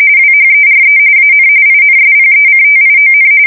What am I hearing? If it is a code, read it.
RTTY, about 45.45 baud. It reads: WHERE AN INSPECTION